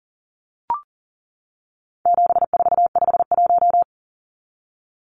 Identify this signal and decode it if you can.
Morse code, heard as a keyed tone, about 40 wpm, 700 Hz. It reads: 7451
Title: Numbers - 4-Digit 40wpm